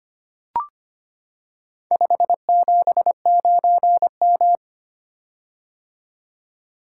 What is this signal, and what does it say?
Morse code, heard as a keyed tone, about 25 wpm, 700 Hz. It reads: 579M
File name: practice / 25wpm / Four Letter-Number Combinations - 25wpm